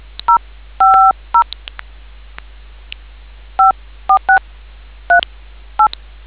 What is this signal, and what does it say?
DTMF, heard as telephone touch tones, tones of about 110 ms. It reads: 050 54638